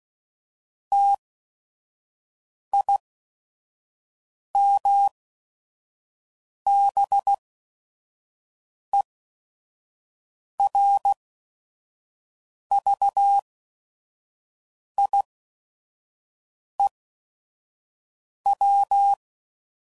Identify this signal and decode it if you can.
Morse code, heard as a keyed tone, about 16 wpm, spaced as for 5 wpm, 800 Hz. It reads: TIMBERVIEW
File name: CW Code Groups @5WPM / fromSM's / Cities